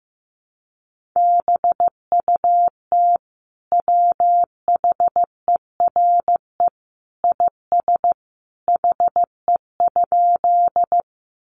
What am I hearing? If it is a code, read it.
Morse code, heard as a keyed tone, about 15 wpm, 700 Hz. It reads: BUT WHERE IS HE?
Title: Sentences from Top 200 Words 15wpm